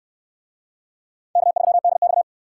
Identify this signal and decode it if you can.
Morse code, heard as a keyed tone, about 63 wpm, 700 Hz. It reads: K3DX